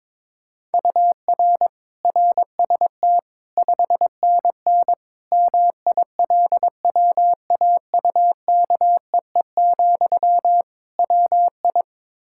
Morse code, heard as a keyed tone, about 22 wpm, 700 Hz. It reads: UR RST 5NN MILWAUKEE, WI
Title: K1USN SST QSO Practice 22wpm